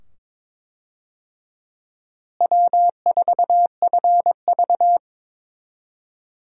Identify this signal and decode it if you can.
Morse code, heard as a keyed tone, about 22 wpm, 700 Hz. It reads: W4FV
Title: USA Call Signs - 4 Characters - Repeated 2x - 22wpm